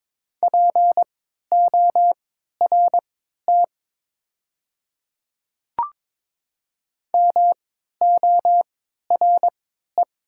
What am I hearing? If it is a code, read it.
Morse code, heard as a keyed tone, about 22 wpm, spaced as for 12 wpm, 700 Hz. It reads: PORT MORE
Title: 3x Character Spacing - Top 200 Words - 22wpm